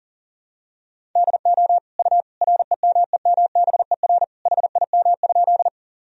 Morse code, heard as a keyed tone, about 40 wpm, 700 Hz. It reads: DO U REMEMBER HIM?
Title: Sentences from Top 400 Words and Abbreviations 40wpm